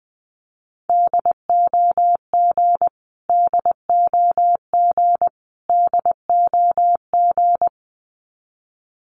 Morse code, heard as a keyed tone, about 20 wpm, 700 Hz. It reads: DOG DOG DOG
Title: Top 500 Words Repeated 20wpm